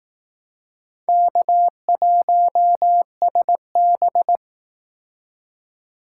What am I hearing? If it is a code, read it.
Morse code, heard as a keyed tone, about 18 wpm, 700 Hz. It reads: K1SB
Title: Speed Racing - USA 1x2 Call Signs 15wpm